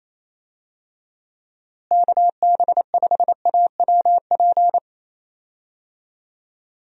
Morse code, heard as a keyed tone, about 27 wpm, 700 Hz. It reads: KB5AWP
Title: USA Call Signs - 6 Characters - Repeated 2x - 28wpm